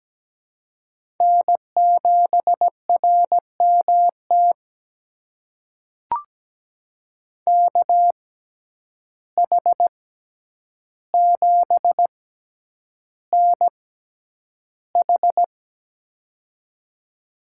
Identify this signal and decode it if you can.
Morse code, heard as a keyed tone, about 17 wpm, spaced as for 9 wpm, 700 Hz. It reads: N7RMT KH7NH
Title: USA Call Signs - 6x Character Spacing - 17wpm